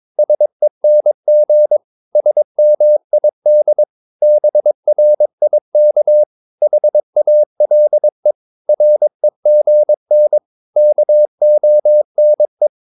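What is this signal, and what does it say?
Morse code, heard as a keyed tone, about 22 wpm, 600 Hz. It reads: SENG SMID BRIK HALE REGN KONE
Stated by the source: Morse practice of short Danish words at 22 wpm, from CW med Gnister.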